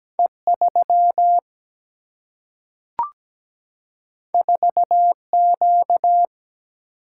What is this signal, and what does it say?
Morse code, heard as a keyed tone, about 17 wpm, 700 Hz. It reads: E3 4Q
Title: Single-Letter Single-Number 17wpm